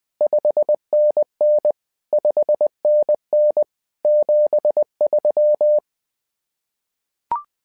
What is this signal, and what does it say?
Morse code, heard as a keyed tone, about 20 wpm, 600 Hz. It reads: 5NN 5NN 73
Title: SOTA QSO Practice 20wpm